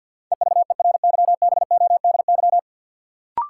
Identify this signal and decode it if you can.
Morse code, heard as a keyed tone, about 47 wpm, 700 Hz. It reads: EVERYBODY
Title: Sentences from Top 1400 Words - New Word Drill Alt - 50wpm